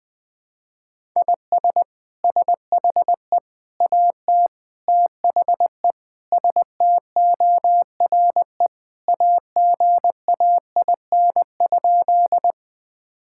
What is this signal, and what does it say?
Morse code, heard as a keyed tone, about 21 wpm, 700 Hz. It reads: IS SHE AT THE STORE AGAIN?
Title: Sentences from Top 1000 Words - Review Part 5 - 20wpm